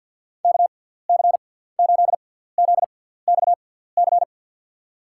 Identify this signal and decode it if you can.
Morse code, heard as a keyed tone, about 49 wpm, spaced as for 17 wpm, 700 Hz. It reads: KC8ZXC
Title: USA Call Signs - 6 Characters - 6x Character Spacing - 50wpm